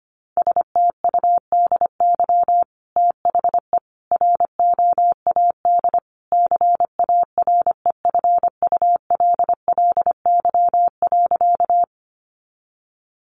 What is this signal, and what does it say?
Morse code, heard as a keyed tone, about 25 wpm, 700 Hz. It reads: STUDY THE ROAD CAREFULLY.
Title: Sentences from Top 1300 Words 25wpm